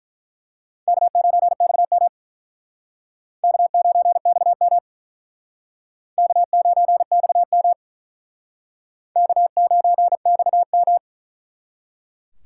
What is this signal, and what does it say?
Morse code, heard as a keyed tone, about 47 wpm, 700 Hz. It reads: K9XM K9XM K9XM K9XM